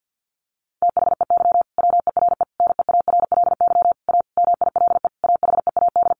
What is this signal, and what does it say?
Morse code, heard as a keyed tone, about 50 wpm, 700 Hz. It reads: THEY WERE NEARLY A MILE AHEAD
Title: Sentences from Top 1300 Words 50wpm